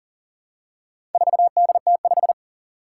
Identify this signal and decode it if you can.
Morse code, heard as a keyed tone, about 38 wpm, 700 Hz. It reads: 4DT5
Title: Four Challenging Letter-Number Combinations 40wpm